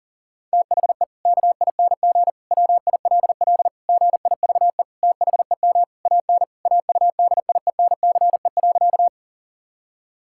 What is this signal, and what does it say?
Morse code, heard as a keyed tone, about 40 wpm, 700 Hz. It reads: THE KING WILL GIVE THEM AN AUDIENCE.
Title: Sentences from Top 1000 Words - Review Part 5 - 40wpm